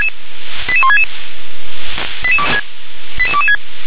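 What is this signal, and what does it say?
Five-tone selective call (ZVEI), heard as quick runs of five tones, tones of about 70 ms. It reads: 88166 88226
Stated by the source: ZVEI five-tone selective call received on an SDR.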